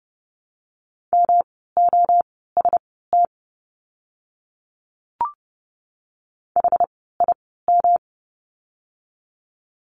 Morse code, heard as a keyed tone, about 30 wpm, spaced as for 17 wpm, 700 Hz. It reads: MOST HIM
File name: course / 30:10wpm / 230 Lesson 37f - Letter x - Words 30wpm